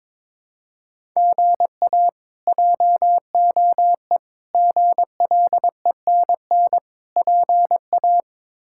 Morse code, heard as a keyed tone, about 22 wpm, 700 Hz. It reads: GA JOE GLENN PA
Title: K1USN SST QSO Practice - 22wpm